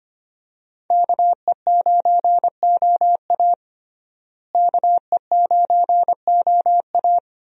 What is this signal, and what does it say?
Morse code, heard as a keyed tone, about 25 wpm, 700 Hz. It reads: KE9OA KE9OA
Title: USA Call Signs - 5 Characters - Repeated 3x - 25wpm